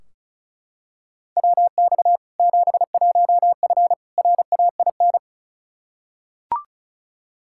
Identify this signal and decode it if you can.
Morse code, heard as a keyed tone, about 34 wpm, 700 Hz. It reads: WX 71F RAIN
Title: K1USN SST QSO Practice 35wpm